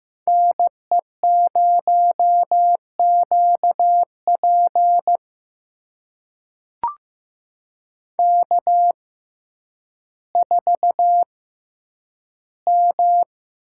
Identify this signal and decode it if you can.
Morse code, heard as a keyed tone, about 15 wpm, spaced as for 9 wpm, 700 Hz. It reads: NE0QP K4M
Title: USA Call Signs - 5 Characters - 6x Character Spacing - 15wpm